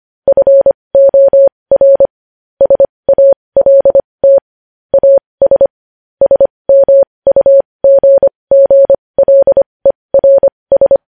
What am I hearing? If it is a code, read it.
Morse code, heard as a keyed tone, about 25 wpm, spaced as for 20 wpm, 550 Hz. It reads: FOR SALT AS SMUGGLERS
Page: News Headlines in Morse Code at 20 WPM